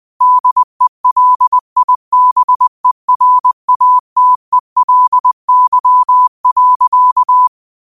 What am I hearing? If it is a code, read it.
Morse code, heard as a keyed tone, about 20 wpm, 1000 Hz. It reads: DELIBERATELY.
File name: qotdfiles / 20wpm